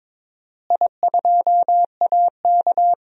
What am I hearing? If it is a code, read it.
Morse code, heard as a keyed tone, about 22 wpm, 700 Hz. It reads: I2AK
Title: DXCC-100 Call Signs - 4 Characters - Repeated 3x - 22wpm